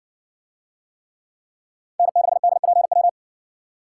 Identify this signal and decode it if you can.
Morse code, heard as a keyed tone, about 57 wpm, 700 Hz. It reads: N6DPW